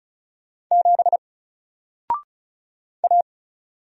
Morse code, heard as a keyed tone, about 36 wpm, 700 Hz. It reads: 7 A
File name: Single-Letter-Number-35wpm